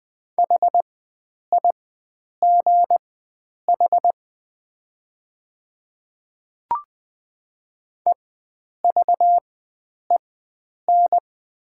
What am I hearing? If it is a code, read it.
Morse code, heard as a keyed tone, about 20 wpm, spaced as for 9 wpm, 700 Hz. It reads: HIGH EVEN
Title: 4x Character Spacing - Top 300 Words - 20wpm